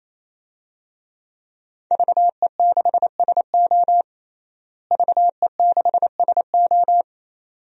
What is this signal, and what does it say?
Morse code, heard as a keyed tone, about 28 wpm, 700 Hz. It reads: VE6SO VE6SO
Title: DXCC-100 Call Signs - 5 Characters - Repeated 3x - 28wpm